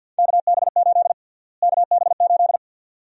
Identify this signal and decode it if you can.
Morse code, heard as a keyed tone, about 49 wpm, 700 Hz. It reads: KB8 KB8
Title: USA Call Sign Prefixes - Repeated 2x - 50wpm